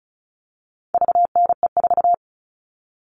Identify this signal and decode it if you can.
Morse code, heard as a keyed tone, about 34 wpm, 700 Hz. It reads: VNE4
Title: All Cut Numbers - 4-Digit 35wpm